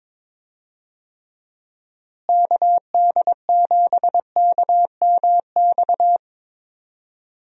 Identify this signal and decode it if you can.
Morse code, heard as a keyed tone, about 21 wpm, 700 Hz. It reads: KD7KMX
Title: USA Call Signs - 6 Characters - Repeated 2x - 22wpm